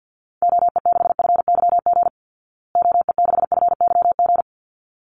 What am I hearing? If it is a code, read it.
Morse code, heard as a keyed tone, about 48 wpm, 700 Hz. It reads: OE6FYG OE6FYG
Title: Call Signs Repeated 3x - 50wpm